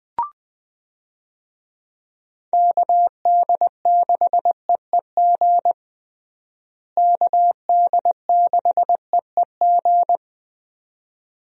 Morse code, heard as a keyed tone, about 20 wpm, 700 Hz. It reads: KD6EEG KD6EEG
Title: USA Call Signs - 6 Characters - Repeated 2x - 20wpm